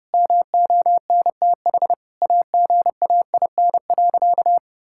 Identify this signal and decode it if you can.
Morse code, heard as a keyed tone, about 30 wpm, 700 Hz. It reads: MONTH AGAIN.
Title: Sentences from Top 1000 Words - Review Part 5 - 30wpm